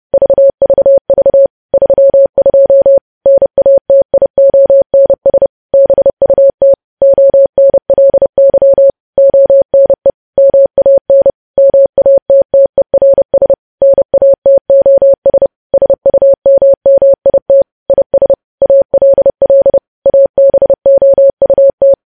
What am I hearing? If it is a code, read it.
Morse code, heard as a keyed tone, about 30 wpm, 550 Hz. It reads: VVV 32 NATIONS BUT ONLY ONE MAN MATTERS NATOS SUMMIT IS ALL ABOUT
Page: news headlines in morse code at 30 wpm